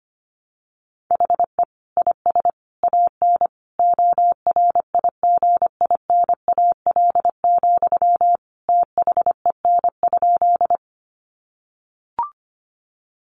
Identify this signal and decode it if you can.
Morse code, heard as a keyed tone, about 25 wpm, 700 Hz. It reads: HE IS AN ORIGINAL, THEN?
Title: Sentences from Top 900 Words 25wpm